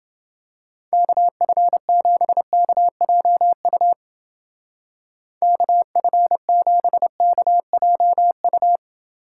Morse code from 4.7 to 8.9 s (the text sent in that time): KF7KJU